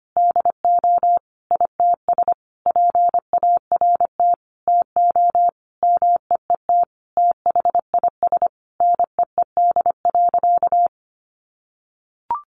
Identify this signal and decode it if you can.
Morse code, heard as a keyed tone, about 25 wpm, 700 Hz. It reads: DO ITS PART TO MEET THIS NEED.
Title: Sentences from Top 1100 Words 25wpm